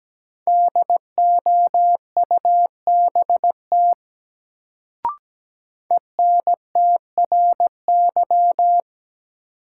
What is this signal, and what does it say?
Morse code, heard as a keyed tone, about 17 wpm, 700 Hz. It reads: DOUBT ENTRY
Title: Sentences from Top 1500 Words - New Word Drill - 17wpm